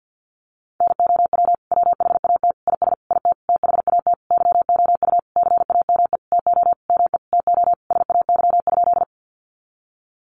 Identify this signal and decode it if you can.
Morse code, heard as a keyed tone, about 50 wpm, 700 Hz. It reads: NOW WHAT IS IT THAT YOU CAME TO ME TO SAY?
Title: Sentences from Top 200 Words 50wpm